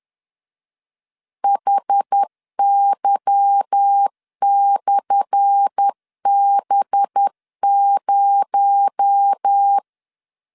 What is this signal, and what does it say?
Morse code, heard as a keyed tone, about 10 wpm, 800 Hz. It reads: HY/B0
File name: LEZIONI / 10 SIGNS / 10A SLASH 10wpm 2808